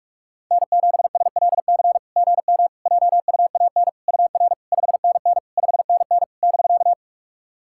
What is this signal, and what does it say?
Morse code, heard as a keyed tone, about 44 wpm, 690 Hz. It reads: N7SLC GM JUAN UR 5NN 5NN <BK>
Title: SOTA QSO Practice 45wpm